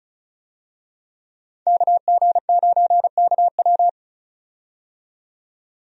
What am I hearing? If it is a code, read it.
Morse code, heard as a keyed tone, about 34 wpm, 700 Hz. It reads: KG9KW
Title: USA Call Signs - 5 Characters - 35wpm